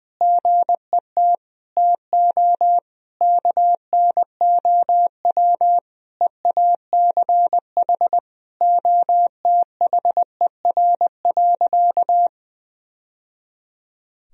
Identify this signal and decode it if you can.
Morse code, heard as a keyed tone, about 20 wpm, 700 Hz. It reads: GET TO KNOW EACH OTHER.
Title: Sentences from Top 1000 Words - Review Part 3 - 20wpm